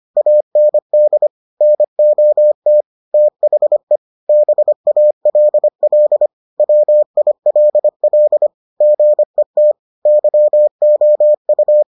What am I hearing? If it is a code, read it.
Morse code, heard as a keyed tone, about 25 wpm, 600 Hz. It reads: AND NOT THE BALL WILL GET YOU